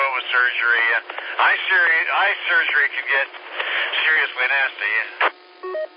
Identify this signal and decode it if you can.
Emergency Alert System (SAME) tones (header bursts), heard as harsh buzzing bursts, with speech. It is heard from 0.3 to 5.0 s.